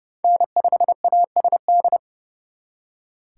Morse code, heard as a keyed tone, about 37 wpm, 700 Hz. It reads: N5ASD
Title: Speed Racing - USA 1x3 Call Signs 30wpm